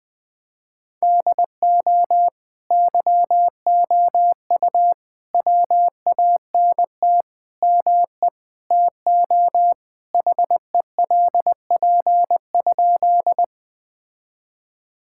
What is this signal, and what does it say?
Morse code, heard as a keyed tone, about 21 wpm, 700 Hz. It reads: DO YOU WANT ME TO HELP?
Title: Sentences with 5-Word Phrases - Spoken Beginning - 20wpm